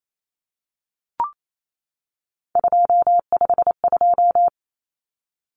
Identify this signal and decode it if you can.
Morse code, heard as a keyed tone, about 28 wpm, 700 Hz. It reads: 252
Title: Common Cut Numbers - 3-Digit 28wpm